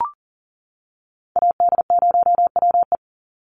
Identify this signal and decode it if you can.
Morse code, heard as a keyed tone, about 40 wpm, 700 Hz. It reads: AD0WE
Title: Call Signs - 40wpm